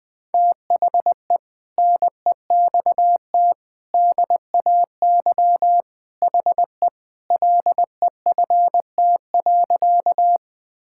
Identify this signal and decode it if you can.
Morse code, heard as a keyed tone, about 20 wpm, 700 Hz. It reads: THE NEXT DAY HE LEFT.
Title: Sentences with 3-Word Phrases - Spoken Beginning - 20wpm